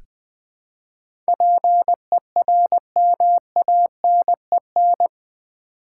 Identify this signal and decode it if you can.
Morse code, heard as a keyed tone, about 20 wpm, 700 Hz. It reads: PERMANEN
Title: Build A Word - 4 Letters At A Time - 11-Letter Words - 20wpm